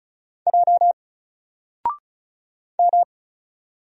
Morse code, heard as a keyed tone, about 35 wpm, 700 Hz. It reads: J M
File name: Single-Letter-Number-35wpm